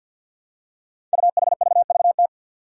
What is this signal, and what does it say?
Morse code, heard as a keyed tone, about 47 wpm, 700 Hz. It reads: UHVVT
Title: Five Challenging Letter-Number Combinations 50wpm